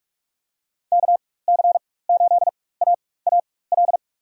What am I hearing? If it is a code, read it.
Morse code, heard as a keyed tone, about 47 wpm, spaced as for 21 wpm, 700 Hz. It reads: KC8AAL
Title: USA Call Signs - 6 Characters - 4x Character Spacing - 45wpm